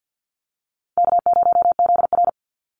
Morse code, heard as a keyed tone, about 50 wpm, 700 Hz. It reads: K0ZR
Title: Easy Call Signs - 50wpm